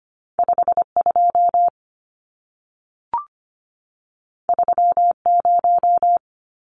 Morse code, heard as a keyed tone, about 25 wpm, 700 Hz. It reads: 52 30
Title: Numbers - 2-Digit 25wpm